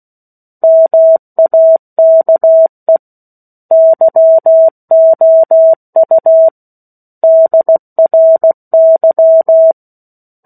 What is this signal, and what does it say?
Morse code, heard as a keyed tone, about 16 wpm, 650 Hz. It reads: MAKE YOU DRY
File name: 16 WPM